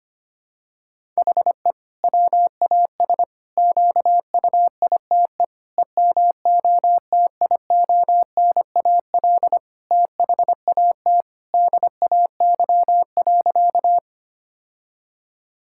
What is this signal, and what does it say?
Morse code, heard as a keyed tone, about 25 wpm, 700 Hz. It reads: HE WAS QUITE EMOTIONAL THAT DAY.
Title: Sentences from Top 1000 Words - Review Part 5 - 25wpm